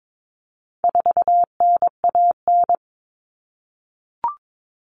Morse code, heard as a keyed tone, about 22 wpm, 700 Hz. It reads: 4NAN
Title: Common Cut Numbers - 4-Digit 22wpm